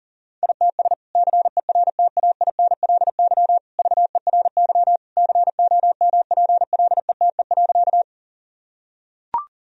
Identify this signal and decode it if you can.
Morse code, heard as a keyed tone, about 40 wpm, 700 Hz. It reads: ITS CERTAINLY VERY COMPLETE.
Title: Sentences from Top 1000 Words - Review Part 2 - 40wpm